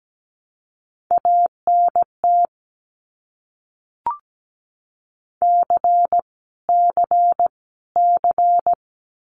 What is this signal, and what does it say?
Morse code, heard as a keyed tone, about 17 wpm, 700 Hz. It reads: ANT C C C
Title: Abbreviations Repeated 3x - 17wpm